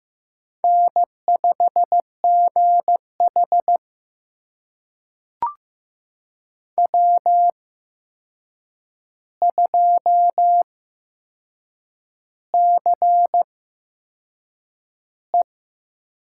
Morse code, heard as a keyed tone, about 15 wpm, spaced as for 6 wpm, 700 Hz. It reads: N5GH W2CE
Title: USA Call Signs - 4 Characters - 8x Character Spacing - 15wpm